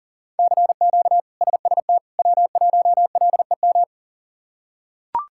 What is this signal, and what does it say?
Morse code, heard as a keyed tone, about 40 wpm, 700 Hz. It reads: CQ SST W1LEM
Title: K1USN SST QSO Practice - 40wpm